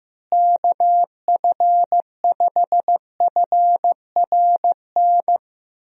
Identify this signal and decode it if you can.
Morse code, heard as a keyed tone, about 15 wpm, 700 Hz. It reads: KF5FRN
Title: USA Call Signs - 6 Characters - Repeated 2x - 15wpm